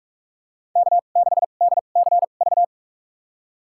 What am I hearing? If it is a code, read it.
Morse code, heard as a keyed tone, about 43 wpm, spaced as for 32 wpm, 700 Hz. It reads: K6DCV
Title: USA Call Signs - 5 Characters - 2x Character Spacing - 45wpm